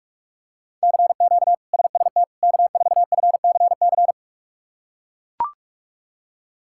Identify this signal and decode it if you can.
Morse code, heard as a keyed tone, about 41 wpm, 700 Hz. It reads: CQ SST K4FCC
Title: K1USN SST QSO Practice - 45wpm